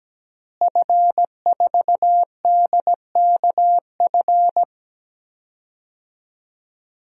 Morse code, heard as a keyed tone, about 17 wpm, 700 Hz. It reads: F4DKF
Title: DXCC-100 Call Signs - 5 Characters - Repeated 2x - 17wpm